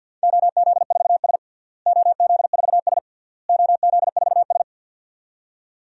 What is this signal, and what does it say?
Morse code, heard as a keyed tone, about 51 wpm, 700 Hz. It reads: OZ4S OZ4S OZ4S
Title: DXCC-100 Call Signs - 4 Characters - Repeated 3x - 50wpm